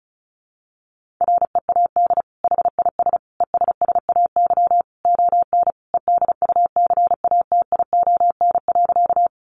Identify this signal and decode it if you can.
Morse code, heard as a keyed tone, about 34 wpm, 700 Hz. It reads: READ HIS ESSAY ON EDUCATION.
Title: Sentences from Top 1500 Words 35wpm